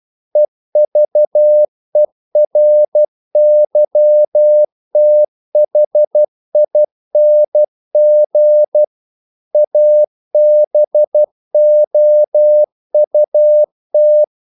Morse code, heard as a keyed tone, about 12 wpm, 600 Hz. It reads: EVERYTHING ABOUT
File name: CW / Practice / EnglishQuotations0112